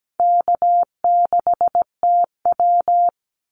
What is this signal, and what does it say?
Morse code, heard as a keyed tone, about 17 wpm, 700 Hz. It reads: K6TW
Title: NA SOTA Activator Callsigns 17wpm